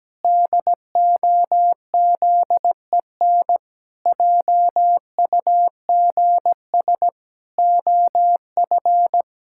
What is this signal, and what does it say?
Morse code, heard as a keyed tone, about 17 wpm, 700 Hz. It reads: DOZEN JUGS OF
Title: Pangram Challenge v2 - 17wpm